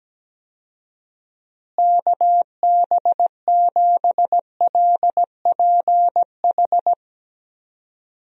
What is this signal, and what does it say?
Morse code, heard as a keyed tone, about 17 wpm, 700 Hz. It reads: KB7LPH
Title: USA Call Signs - 6 Characters - Repeated 3x - 17wpm